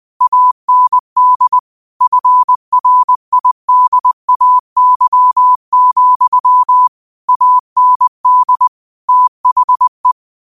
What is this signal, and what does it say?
Morse code, heard as a keyed tone, about 20 wpm, 1000 Hz. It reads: AND FRIDAY, AND THE
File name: qotdfiles / 20wpm